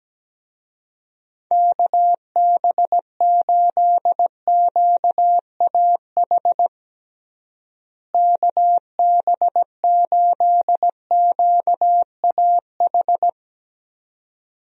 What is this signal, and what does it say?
Morse code, heard as a keyed tone, about 17 wpm, 700 Hz. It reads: KB8QAH KB8QAH
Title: USA Call Signs - 6 Characters - Repeated 2x - 17wpm